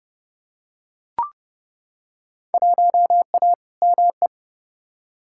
Morse code, heard as a keyed tone, about 30 wpm, 700 Hz. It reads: 1A ME
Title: Field Day Exchanges Encore - 30wpm